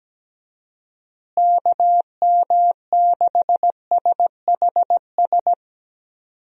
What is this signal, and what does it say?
Morse code, heard as a keyed tone, about 17 wpm, 700 Hz. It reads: KM6SHS
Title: USA Call Signs - 6 Characters - Repeated 2x - 17wpm